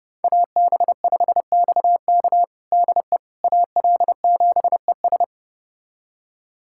Morse code, heard as a keyed tone, about 30 wpm, 700 Hz. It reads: AB5XK DE AL7ES
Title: USA Callsign DE Callsign - 5 Characters - 30wpm